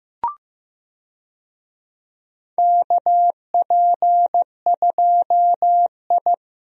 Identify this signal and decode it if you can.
Morse code, heard as a keyed tone, about 15 wpm, 700 Hz. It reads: KP2I
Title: DXCC-100 Call Signs - 4 Characters - 15wpm